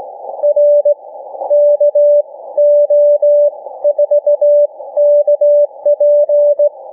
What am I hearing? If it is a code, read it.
Morse code, heard as a keyed tone, about 18 wpm, spaced as for 15 wpm, 590 Hz. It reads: R KO4KP